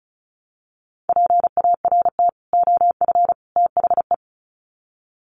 Morse code, heard as a keyed tone, about 34 wpm, 700 Hz. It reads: PART OF THE
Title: Sets of 3 Words 35wpm